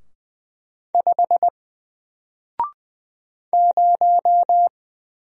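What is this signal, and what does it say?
Morse code, heard as a keyed tone, about 20 wpm, 700 Hz. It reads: 5 0
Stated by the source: Single letter and number Morse drill at 20 wpm, from N4LCD.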